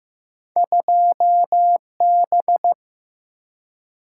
Morse code, heard as a keyed tone, about 15 wpm, 700 Hz. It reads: 2 B
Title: Single-Letter Single-Number 15wpm